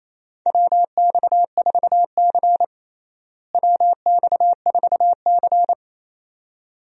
Morse code, heard as a keyed tone, about 28 wpm, 700 Hz. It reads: WX4C WX4C
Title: USA Call Signs - 4 Characters - Repeated 2x - 28wpm